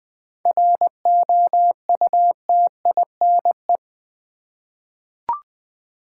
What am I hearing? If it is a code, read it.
Morse code, heard as a keyed tone, about 20 wpm, 700 Hz. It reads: ROUTINE
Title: Seven Letter Words - 20wpm